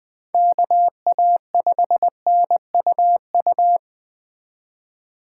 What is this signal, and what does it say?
Morse code, heard as a keyed tone, about 23 wpm, 700 Hz. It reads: KA5NUU
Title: Speed Racing - USA Call Signs 20wpm